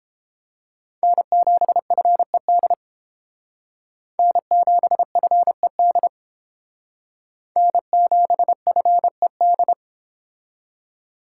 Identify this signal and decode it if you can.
Morse code, heard as a keyed tone, about 30 wpm, 700 Hz. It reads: N7FED N7FED N7FED